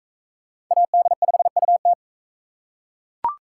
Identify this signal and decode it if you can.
Morse code, heard as a keyed tone, about 38 wpm, 700 Hz. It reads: AD5UT